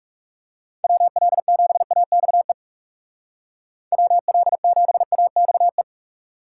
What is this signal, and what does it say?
Morse code, heard as a keyed tone, about 37 wpm, 700 Hz. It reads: WL7AXE WL7AXE
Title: Speed Racing - USA 2x3 Call Signs 30wpm